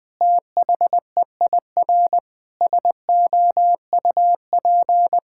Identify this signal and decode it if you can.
Morse code, heard as a keyed tone, about 20 wpm, 700 Hz. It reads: THEIR SOUP
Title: Sentences from Top 1000 Words - Review Part 6 - 20wpm